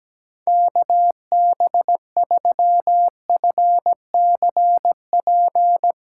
Morse code, heard as a keyed tone, about 17 wpm, 700 Hz. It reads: KB3FCP
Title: USA Callsign DE Callsign 17wpm